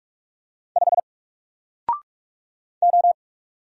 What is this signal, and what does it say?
Morse code, heard as a keyed tone, about 44 wpm, 700 Hz. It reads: 5 O
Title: Single Letter-Number Alt 45wpm